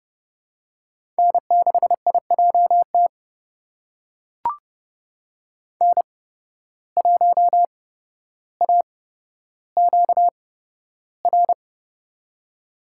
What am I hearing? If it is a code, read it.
Morse code, heard as a keyed tone, about 31 wpm, spaced as for 13 wpm, 700 Hz. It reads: N6IJT N1AQR
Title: USA Call Signs - 5 Characters - 8x Character Spacing - 30wpm